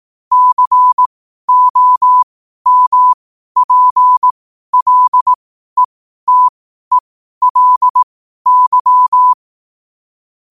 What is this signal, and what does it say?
Morse code, heard as a keyed tone, about 18 wpm, spaced as for 12 wpm, 1000 Hz. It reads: COMPLETELY
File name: qotdfiles / 12wpm